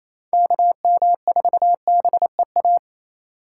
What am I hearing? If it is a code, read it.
Morse code, heard as a keyed tone, about 28 wpm, 700 Hz. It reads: KM4BEA
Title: USA Call Signs - 3x Character Spacing - 28wpm